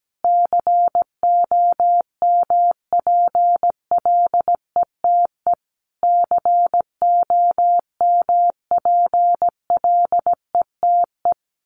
Morse code, heard as a keyed tone, about 17 wpm, 700 Hz. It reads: COMPLETE COMPLETE
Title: Top 500 Words Repeated 17wpm